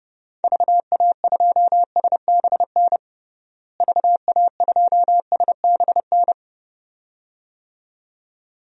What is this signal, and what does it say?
Morse code, heard as a keyed tone, about 29 wpm, 700 Hz. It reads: VA2SBN VA2SBN
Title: DXCC-100 Call Signs - 6 Characters - Repeated 2x - 30wpm